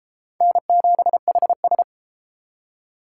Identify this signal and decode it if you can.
Morse code, heard as a keyed tone, about 32 wpm, 700 Hz. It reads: N7HS